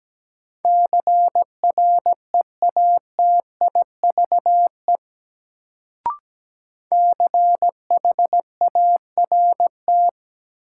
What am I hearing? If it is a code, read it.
Morse code, heard as a keyed tone, about 17 wpm, 700 Hz. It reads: CREATIVE CHART
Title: Sentences from Top 1300 Words - New Word Drill Alt - 17wpm